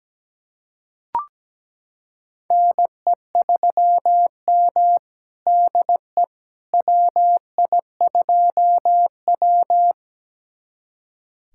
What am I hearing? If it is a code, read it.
Morse code, heard as a keyed tone, about 17 wpm, 700 Hz. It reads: NE3M DE WI2W
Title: USA Callsign DE Callsign - 4 Characters - 17wpm